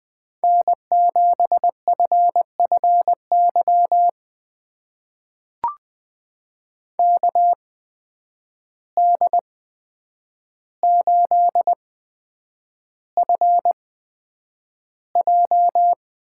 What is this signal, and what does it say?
Morse code, heard as a keyed tone, about 20 wpm, spaced as for 9 wpm, 700 Hz. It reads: N7FFY KD8FJ
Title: USA Call Signs - 8x Character Spacing - 20wpm